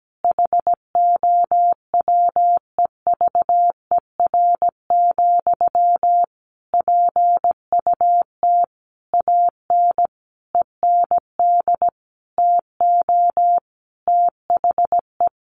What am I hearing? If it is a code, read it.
Morse code, heard as a keyed tone, about 17 wpm, 700 Hz. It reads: HOWEVER, PUT AN END TO THE
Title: Sentences from Top 1200 Words 17wpm